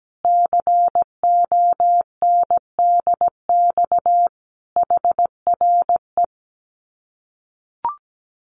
Morse code, heard as a keyed tone, about 17 wpm, 700 Hz. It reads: CONDX HRE
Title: Common QSO Phrases 17wpm